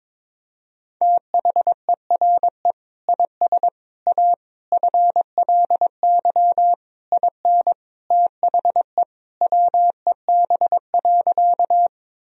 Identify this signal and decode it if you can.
Morse code, heard as a keyed tone, about 22 wpm, 700 Hz. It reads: THERE IS A FLY IN THE WEB.
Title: Sentences from Top 1000 Words - Review Part 4 - 22wpm